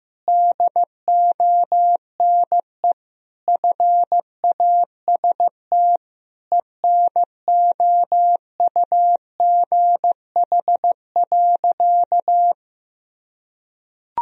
Morse code, heard as a keyed tone, about 15 wpm, 700 Hz. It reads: DONE FAST ENOUGH.
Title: Sentences from Top 1000 Words - Review Part 2 - 15wpm